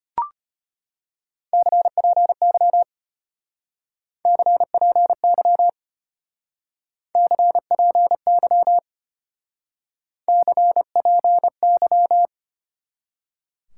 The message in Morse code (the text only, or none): CPY CPY CPY CPY